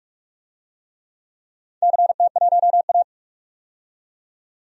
Morse code, heard as a keyed tone, about 44 wpm, 690 Hz. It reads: CT1A
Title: DXCC-100 Call Signs - 4 Characters - 45wpm